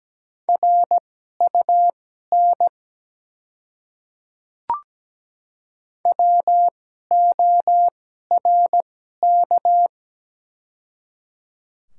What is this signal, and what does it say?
Morse code, heard as a keyed tone, about 17 wpm, spaced as for 12 wpm, 690 Hz. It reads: RUN WORK
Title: 2x Character Spacing - Top 300 Words - 17wpm